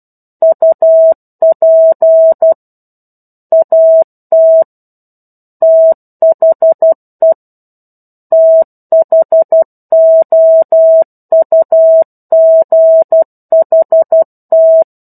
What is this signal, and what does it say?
Morse code, heard as a keyed tone, about 12 wpm, 650 Hz. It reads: UP AT THE THOUGHT
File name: 01-Chapter-01 12 WPM